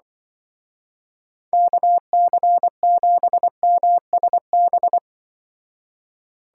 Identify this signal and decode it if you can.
Morse code, heard as a keyed tone, about 24 wpm, 700 Hz. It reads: KC7MSB
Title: Speed Racing - USA 2x3 Call Signs 20wpm